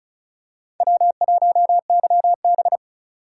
Morse code, heard as a keyed tone, about 36 wpm, 700 Hz. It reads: W1YB
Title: USA Call Signs - 4 Characters - 4x Character Spacing - 35wpm